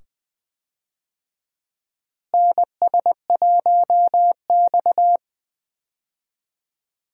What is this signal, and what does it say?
Morse code, heard as a keyed tone, about 20 wpm, 700 Hz. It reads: NS1X
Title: USA Call Signs - 4 Characters - Repeated 2x - 20wpm